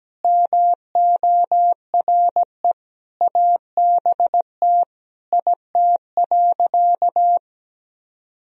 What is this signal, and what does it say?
Morse code, heard as a keyed tone, about 17 wpm, 700 Hz. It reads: MORE ABT IT.
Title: Sentences from Top 200 Words and Abbreviations 17wpm